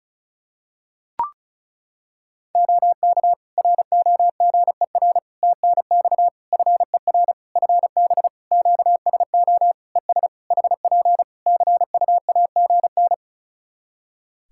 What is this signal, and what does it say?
Morse code, heard as a keyed tone, about 34 wpm, 700 Hz. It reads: OK ROGER TNX FER FB QSO ES HP CUAGN
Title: Practice QSO - Exchange 4 - Ending - 35wpm